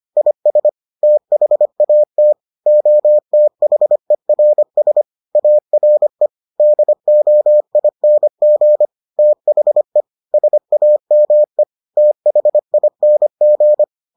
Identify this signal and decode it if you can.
Morse code, heard as a keyed tone, about 25 wpm, 600 Hz. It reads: IS THAT OTHERS ARE DOING THE SAME THING